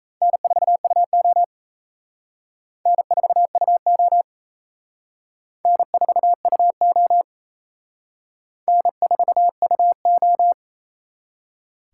Morse code, heard as a keyed tone, about 34 wpm, 700 Hz. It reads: N4UO N4UO N4UO N4UO
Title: Speed Racing - USA 1x2 Call Signs 28wpm